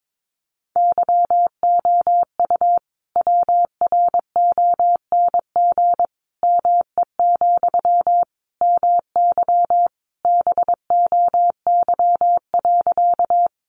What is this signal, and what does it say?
Morse code, heard as a keyed tone, about 22 wpm, 700 Hz. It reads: YOU WRONG ME, MY BOY.
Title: Sentences from Top 900 Words 22wpm